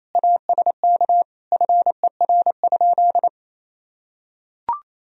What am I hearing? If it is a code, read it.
Morse code, heard as a keyed tone, about 28 wpm, 700 Hz. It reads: ASK FER?
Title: Sentences from Top 200 Words and Abbreviations 28wpm